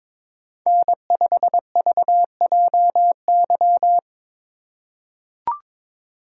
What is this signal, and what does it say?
Morse code, heard as a keyed tone, about 21 wpm, 700 Hz. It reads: N5VJY
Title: USA Call Signs - 6x Character Spacing - 22wpm